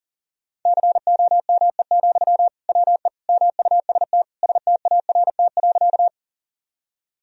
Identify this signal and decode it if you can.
Morse code, heard as a keyed tone, about 40 wpm, 700 Hz. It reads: COME, WE MUST START.
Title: Sentences from Top 1000 Words - Review Part 1 - 40wpm